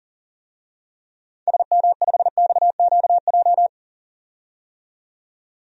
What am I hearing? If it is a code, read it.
Morse code, heard as a keyed tone, about 37 wpm, 700 Hz. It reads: SM5XQJ
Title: DXCC-100 Call Signs - 6 Characters - 40wpm